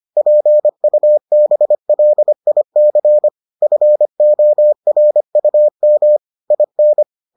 Morse code, heard as a keyed tone, about 25 wpm, 600 Hz. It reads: PUBLIC FORUM IN